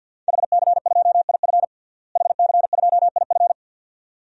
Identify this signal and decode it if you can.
Morse code, heard as a keyed tone, about 51 wpm, 700 Hz. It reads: HC2IF HC2IF
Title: DXCC-100 Call Signs - 5 Characters - Repeated 2x - 50wpm